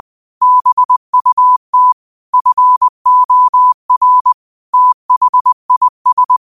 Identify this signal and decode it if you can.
Morse code, heard as a keyed tone, about 20 wpm, 1000 Hz. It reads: BUT FOR THIS